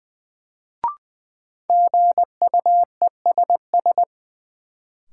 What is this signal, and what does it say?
Morse code, heard as a keyed tone, about 20 wpm, 700 Hz. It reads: GUESS
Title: Sentences from Top 1500 Words - New Word Drill - 20wpm